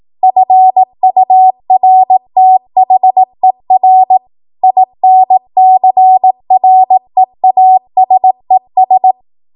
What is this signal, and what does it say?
Morse code, heard as a keyed tone, about 18 wpm, 750 Hz. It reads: FURTHER INCREASES
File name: Morse / 18WPM